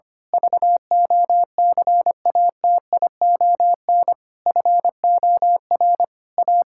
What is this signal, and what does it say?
Morse code, heard as a keyed tone, about 25 wpm, 700 Hz. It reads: VOCATION FOR A
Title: Pangram Challenge v2 - 25wpm